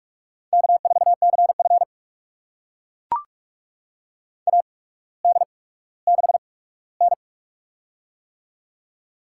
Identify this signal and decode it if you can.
Morse code, heard as a keyed tone, about 44 wpm, spaced as for 19 wpm, 700 Hz. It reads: K4CF AD6N
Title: USA Call Signs - 4 Characters - 8x Character Spacing - 45wpm